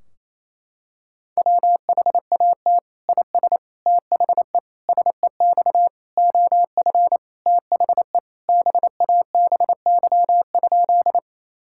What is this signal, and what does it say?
Morse code, heard as a keyed tone, about 28 wpm, 700 Hz. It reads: WHAT IS THE SEX OF THE BABY?
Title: Sentences from Top 1600 Words 28wpm